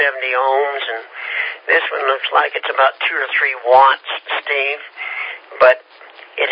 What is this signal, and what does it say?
Emergency Alert System (SAME) tones (header bursts), heard as harsh buzzing bursts, with speech. It is heard from 1.1 to 5.4 s.